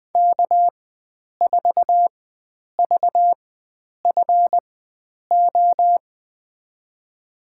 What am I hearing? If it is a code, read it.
Morse code, heard as a keyed tone, about 20 wpm, spaced as for 9 wpm, 700 Hz. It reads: K4VFO
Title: USA Call Signs - 4x Character Spacing - 20wpm